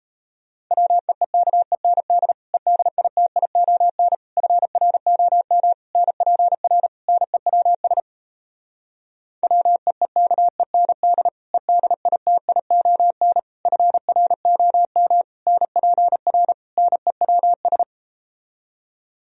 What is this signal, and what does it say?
Morse code, heard as a keyed tone, about 38 wpm, 700 Hz. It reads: WEEKEND EDITION FROM NPR NEWS WEEKEND EDITION FROM NPR NEWS